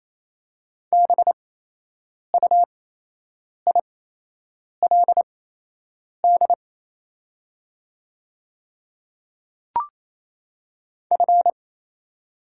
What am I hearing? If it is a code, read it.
Morse code, heard as a keyed tone, about 27 wpm, spaced as for 8 wpm, 690 Hz. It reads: BUILD F
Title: 8x Character Spacing - Top 500 Words - 28wpm